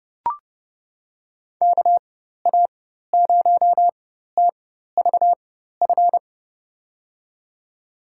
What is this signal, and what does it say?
Morse code, heard as a keyed tone, about 32 wpm, spaced as for 14 wpm, 700 Hz. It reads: KA0TVF
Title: USA Call Signs - 6 Characters - 4x Character Spacing - 30wpm